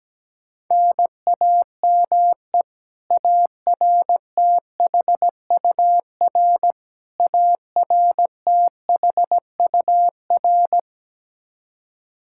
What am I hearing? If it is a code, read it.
Morse code, heard as a keyed tone, about 17 wpm, 700 Hz. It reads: NAME ARTHUR ARTHUR
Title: Practice QSO - Exchange 1 - Beginning - 17wpm